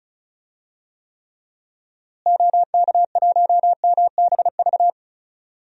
Morse code, heard as a keyed tone, about 36 wpm, 700 Hz. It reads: OK1MBV
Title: DXCC-100 Call Signs - 6 Characters - Repeated 3x - 35wpm